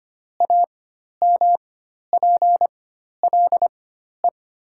Morse code, heard as a keyed tone, about 25 wpm, spaced as for 12 wpm, 700 Hz. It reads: AMPLE
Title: 4x Character Spacing - Top 300 Words - 25wpm